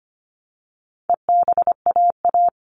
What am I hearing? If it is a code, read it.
Morse code, heard as a keyed tone, about 25 wpm, 700 Hz. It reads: EBAA
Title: All Cut Numbers - 4-Digit 25wpm